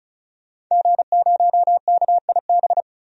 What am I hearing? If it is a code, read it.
Morse code, heard as a keyed tone, about 36 wpm, 700 Hz. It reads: G0KIB
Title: DXCC-100 Call Signs - 5 Characters - 35wpm